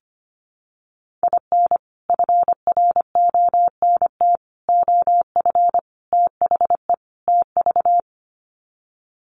Morse code, heard as a keyed tone, about 25 wpm, 700 Hz. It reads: IN FRONT OF THE TV
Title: Sets of 5 Words Encore2 25wpm